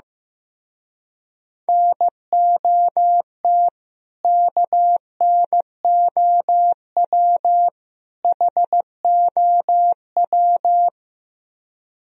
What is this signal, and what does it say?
Morse code, heard as a keyed tone, about 15 wpm, 700 Hz. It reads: NOT KNOW HOW
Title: Speed-Racing - Sets of 3 Words 15wpm